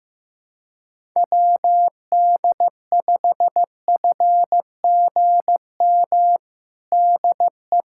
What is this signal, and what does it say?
Morse code, heard as a keyed tone, about 15 wpm, 700 Hz. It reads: WD5FGM DE
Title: USA Callsign DE Callsign - 6 Characters - 15wpm